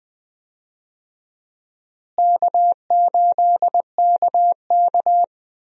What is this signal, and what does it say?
Morse code, heard as a keyed tone, about 20 wpm, 700 Hz. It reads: K8KK
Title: USA Call Signs - 4 Characters - Repeated 2x - 20wpm